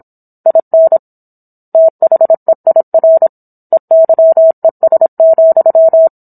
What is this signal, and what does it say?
Morse code, heard as a keyed tone, about 27 wpm, 650 Hz. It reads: IN THEIR EYES,